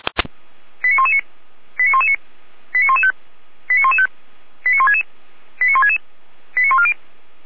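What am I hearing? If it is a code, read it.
Five-tone selective call (ZVEI), heard as quick runs of five tones, tones of about 70 ms. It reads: 89119 89119 89115 89115 89166 89166 89140
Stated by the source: ZVEI five-tone call received on SDR, fire brigade Orlamünde.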